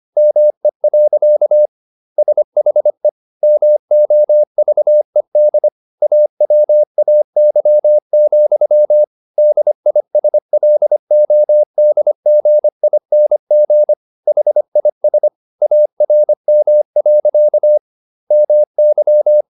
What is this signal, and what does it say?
Morse code, heard as a keyed tone, about 25 wpm, 600 Hz. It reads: ME. SHE MOVED AWAY, DISLODGING HIS ARM. MY